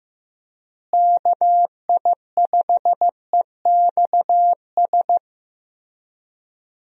Morse code, heard as a keyed tone, about 15 wpm, 700 Hz. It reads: KI5EXS
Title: USA Call Signs - 6 Characters - Repeated 3x - 15wpm